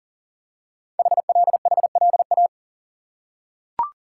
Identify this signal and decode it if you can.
Morse code, heard as a keyed tone, about 40 wpm, 700 Hz. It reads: HLHLA
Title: Code Groups - 5 Letters - 40wpm